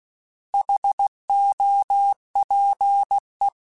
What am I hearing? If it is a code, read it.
Morse code, heard as a keyed tone, about 16 wpm, 800 Hz. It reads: HOPE
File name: Alphabetical / Alphabetical mp3s / Hs @15